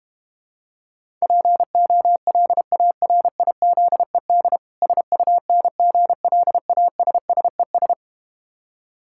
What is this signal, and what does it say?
Morse code, heard as a keyed tone, about 32 wpm, 700 Hz. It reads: POLARIZED SUNGLASSES